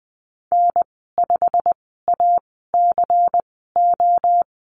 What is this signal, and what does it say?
Morse code, heard as a keyed tone, about 20 wpm, spaced as for 15 wpm, 700 Hz. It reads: N5ACO
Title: 161 Lesson 26g - Letter g - Call Signs 20wpm